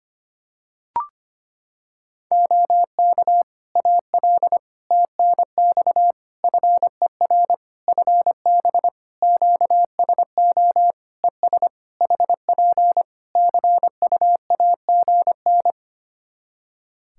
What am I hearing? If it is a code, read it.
Morse code, heard as a keyed tone, about 25 wpm, 700 Hz. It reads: OK AL TNX FER FB QSO ES HP CUAGN
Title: Practice QSO - Exchange 4 - Ending - 25wpm